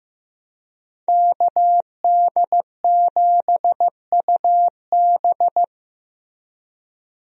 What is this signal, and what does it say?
Morse code, heard as a keyed tone, about 15 wpm, 700 Hz. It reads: KD7UB
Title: USA Call Signs - 5 Characters - Repeated 2x - 15wpm